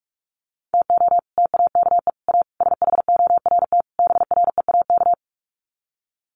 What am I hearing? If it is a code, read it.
Morse code, heard as a keyed tone, about 44 wpm, 690 Hz. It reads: TO TAKE A SHORT BREAK
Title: Sets of 5 Words Encore2 45wpm